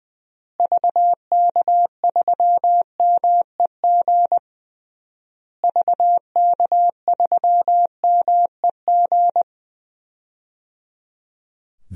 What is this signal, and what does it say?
Morse code, heard as a keyed tone, about 19 wpm, 700 Hz. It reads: VK3MEG VK3MEG
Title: DXCC-100 Call Signs - 6 Characters - Repeated 3x - 20wpm